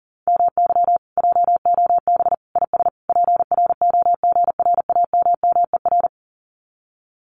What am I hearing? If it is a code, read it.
Morse code, heard as a keyed tone, about 38 wpm, 700 Hz. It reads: MY JOB IS PROGRAMMER
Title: Common QSO Phrases Encore 40wpm